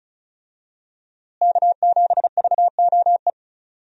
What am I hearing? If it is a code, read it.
Morse code, heard as a keyed tone, about 34 wpm, 700 Hz. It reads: K7VOE
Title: USA Call Signs - 5 Characters - Repeated 2x - 35wpm